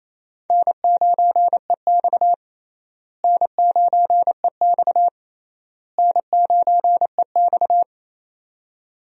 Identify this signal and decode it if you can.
Morse code, heard as a keyed tone, about 28 wpm, 700 Hz. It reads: N9EX N9EX N9EX
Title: USA Call Signs - 4 Characters - Repeated 3x - 28wpm